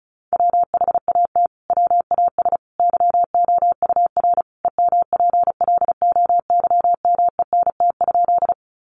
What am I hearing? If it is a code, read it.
Morse code, heard as a keyed tone, about 34 wpm, 700 Hz. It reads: WHAT WAS YOUR EMPLOYMENT?
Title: Sentences from Top 1500 Words 35wpm